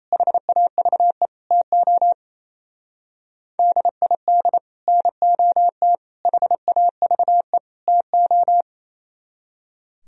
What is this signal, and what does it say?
Morse code, heard as a keyed tone, about 33 wpm, 700 Hz. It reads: HAVE TO DID NOT HAVE TO
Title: Speed-Racing - Sets of 4 Words 28wpm